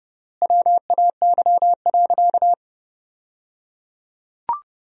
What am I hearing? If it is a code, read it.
Morse code, heard as a keyed tone, about 29 wpm, 700 Hz. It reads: WAY.
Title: 3x Word-Spacing - Sentences from Top 200 Words - 30wpm